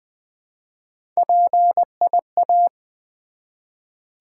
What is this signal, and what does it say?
Morse code, heard as a keyed tone, about 20 wpm, 700 Hz. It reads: PIA
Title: Build A Word - 1 Letter At A Time - 5-Letter Words - 20wpm